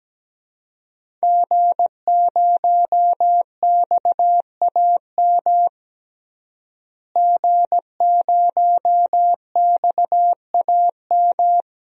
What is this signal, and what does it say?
Morse code, heard as a keyed tone, about 17 wpm, 700 Hz. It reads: G0XAM G0XAM
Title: DXCC-100 Call Signs - 5 Characters - Repeated 3x - 17wpm